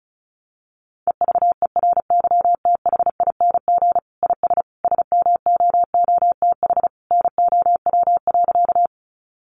Morse code, heard as a keyed tone, about 34 wpm, 700 Hz. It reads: EVERYTHING IS SMOOTH NOW.
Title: Sentences from Top 1400 Words 35wpm